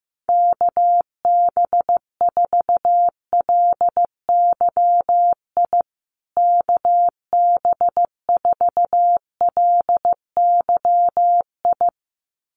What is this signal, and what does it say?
Morse code, heard as a keyed tone, about 15 wpm, 700 Hz. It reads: KB4LYI KB4LYI
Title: USA Call Signs Repeated 2x 15wpm